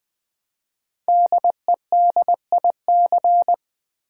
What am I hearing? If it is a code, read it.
Morse code, heard as a keyed tone, about 20 wpm, 700 Hz. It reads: DEDIC
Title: Build A Word - 5 Letters At A Time - 10-Letter Words - 20wpm